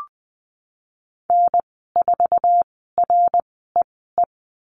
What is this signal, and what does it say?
Morse code, heard as a keyed tone, about 20 wpm, spaced as for 15 wpm, 700 Hz. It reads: N4REE
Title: 062 Lesson 11g - Letter h - Call Signs 20wpm